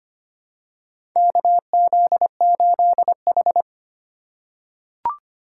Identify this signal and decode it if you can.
Morse code, heard as a keyed tone, about 25 wpm, 700 Hz. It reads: KZ8H
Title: USA Call Signs - 4x Character Spacing - 25wpm